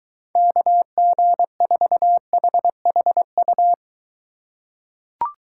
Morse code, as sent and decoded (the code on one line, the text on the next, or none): -.- --. ....- .... .... ..-
KG4HHU